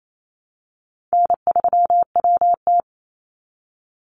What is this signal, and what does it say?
Morse code, heard as a keyed tone, about 28 wpm, 700 Hz. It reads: N3WT
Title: Easy Call Signs - 28wpm